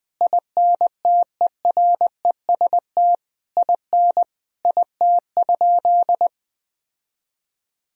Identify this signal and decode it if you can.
Morse code, heard as a keyed tone, about 20 wpm, 700 Hz. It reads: INTEREST IN IT?
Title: Sentences from Top 1000 Words - Review Part 4 - 20wpm